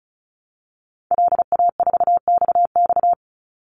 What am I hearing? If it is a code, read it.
Morse code, heard as a keyed tone, about 34 wpm, 700 Hz. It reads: LA4XX
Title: Call Signs Repeated 2x - 35wpm